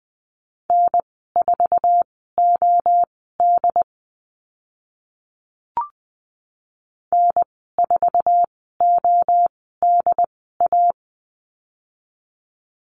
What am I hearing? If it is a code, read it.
Morse code, heard as a keyed tone, about 20 wpm, spaced as for 15 wpm, 700 Hz. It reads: N4OD N4ODA
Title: 088 Lesson 15g - Number 5 - Call Signs 20wpm